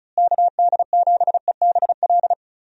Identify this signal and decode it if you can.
Morse code, heard as a keyed tone, about 36 wpm, 700 Hz. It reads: KD7EBL
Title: USA Call Signs - 6 Characters - Repeated 3x - 35wpm